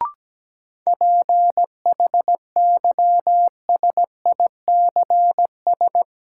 Morse code, heard as a keyed tone, about 17 wpm, 700 Hz. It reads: PHYSICS
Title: Sentences from Top 1000 Words - Review Part 5 - New Word Drill - 17wpm